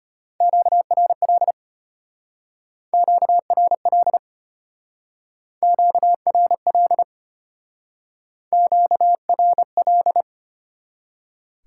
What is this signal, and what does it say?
Morse code, heard as a keyed tone, about 32 wpm, 690 Hz. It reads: QRL QRL QRL QRL